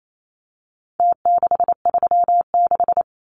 Morse code, heard as a keyed tone, about 28 wpm, 700 Hz. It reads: T636
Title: Numbers - Contest Style - 4-Digit 28wpm